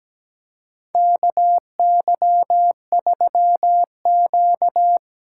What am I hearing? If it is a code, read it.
Morse code, heard as a keyed tone, about 17 wpm, 700 Hz. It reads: KY3Q
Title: USA Call Signs - 4 Characters - 3x Character Spacing - 17wpm